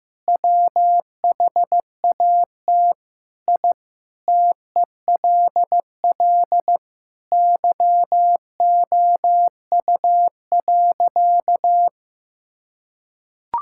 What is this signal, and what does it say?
Morse code, heard as a keyed tone, about 15 wpm, 700 Hz. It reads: WHAT I TELL YOU.
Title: Sentences from Top 1000 Words - Review Part 6 - 15wpm